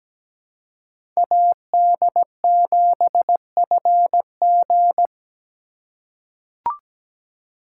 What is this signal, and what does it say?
Morse code, heard as a keyed tone, about 17 wpm, 700 Hz. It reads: AD7FG
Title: USA Call Signs - 4x Character Spacing - 17wpm